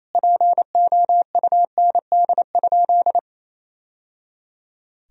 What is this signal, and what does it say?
Morse code, heard as a keyed tone, about 28 wpm, 700 Hz. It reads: POUND?
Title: Sentences from Top 1000 Words - Review Part 2 - 28wpm